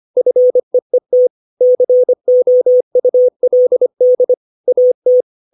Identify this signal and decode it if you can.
Morse code, heard as a keyed tone, about 25 wpm, 500 Hz. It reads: FEET COULD AT
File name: 632 words / 632 Words 25WPM